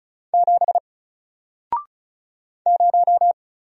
Morse code, heard as a keyed tone, about 34 wpm, 700 Hz. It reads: 7 0
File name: Single-Letter-Number-35wpm